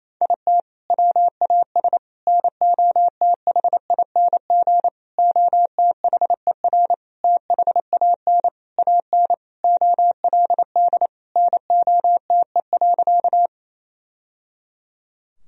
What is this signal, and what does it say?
Morse code, heard as a keyed tone, about 28 wpm, 700 Hz. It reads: IT WAS NOTHING OTHER THAN AN OLD NOTE.
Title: Sentences from Top 1000 Words - Review Part 2 - 28wpm